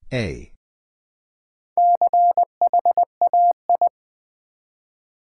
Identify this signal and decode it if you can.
Morse code, heard as a keyed tone, about 20 wpm, 700 Hz. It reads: CHAI
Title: Build A Word - 1 Letter At A Time - 3 to 5-Letter Words - 20wpm